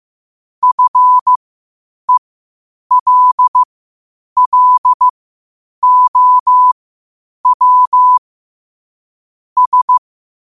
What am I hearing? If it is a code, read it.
Morse code, heard as a keyed tone, about 15 wpm, spaced as for 8 wpm, 1000 Hz. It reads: FELLOW S
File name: qotdfiles / 8wpm